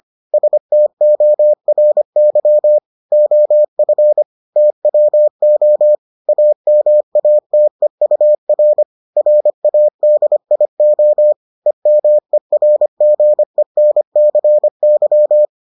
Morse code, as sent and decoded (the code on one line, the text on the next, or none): ... - --- .-. -.-- / --- ..-. / - .-- --- / .- -- .- - . ..- .-. / .-. .- -.. .. --- / . -- . .-. --. -. -.-. -.--
STORY OF TWO AMATEUR RADIO EMERGENCY